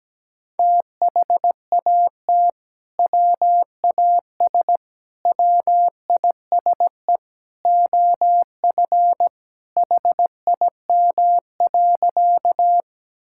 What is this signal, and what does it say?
Morse code, heard as a keyed tone, about 17 wpm, 700 Hz. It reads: THAT WAS WISE OF HIM.
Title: Sentences from Top 1600 Words 17wpm